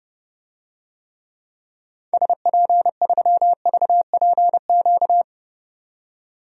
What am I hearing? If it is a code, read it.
Morse code, heard as a keyed tone, about 30 wpm, 700 Hz. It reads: SP3VPQ
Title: DXCC-100 Call Signs - 6 Characters - 30wpm